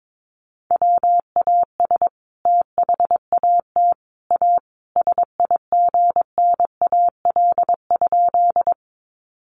Morse code, heard as a keyed tone, about 22 wpm, 700 Hz. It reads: WAS THAT A SIGNAL?
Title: Sentences from Top 1400 Words 22wpm